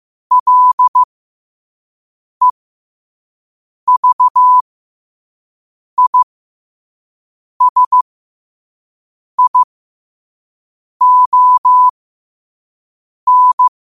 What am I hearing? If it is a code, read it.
Morse code, heard as a keyed tone, about 15 wpm, spaced as for 5 wpm, 1000 Hz. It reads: LEVISION
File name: qotdfiles / 5wpm